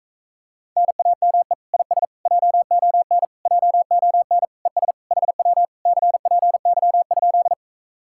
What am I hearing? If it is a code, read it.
Morse code, heard as a keyed tone, about 45 wpm, 700 Hz. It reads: NAME IS JON JON ES HW CPY?